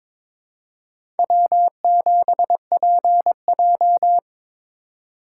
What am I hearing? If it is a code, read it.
Morse code, heard as a keyed tone, about 23 wpm, 700 Hz. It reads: W7PJ
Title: USA Call Signs - 4 Characters - 4x Character Spacing - 22wpm